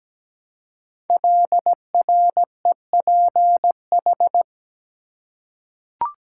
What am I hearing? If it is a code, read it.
Morse code, heard as a keyed tone, about 17 wpm, 700 Hz. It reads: LREPH
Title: Code Groups - 5 Letters - 17wpm